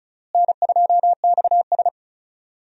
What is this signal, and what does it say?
Morse code, heard as a keyed tone, about 36 wpm, 700 Hz. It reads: N2XS
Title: USA Call Signs - 4 Characters - Repeated 2x - 35wpm